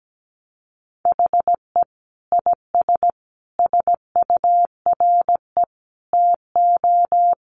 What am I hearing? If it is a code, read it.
Morse code, heard as a keyed tone, about 17 wpm, 700 Hz. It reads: HE IS SURE TO
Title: Sentences from Top 700 Words 17wpm